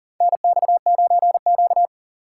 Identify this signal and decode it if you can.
Morse code, heard as a keyed tone, about 40 wpm, 700 Hz. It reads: NX9Q
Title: USA Call Signs - 4 Characters - 2x Character Spacing - 40wpm